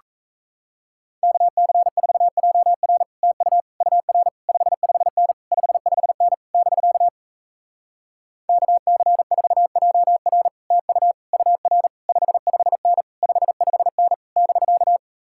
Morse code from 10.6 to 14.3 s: TU UR 55N 55N

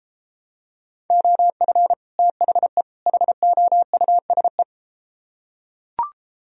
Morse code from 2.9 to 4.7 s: HOUSE

